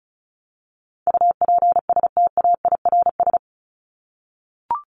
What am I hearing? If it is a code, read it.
Morse code, heard as a keyed tone, about 34 wpm, 700 Hz. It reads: UPSTAIRS
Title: Top 1751-2000 Words 35wpm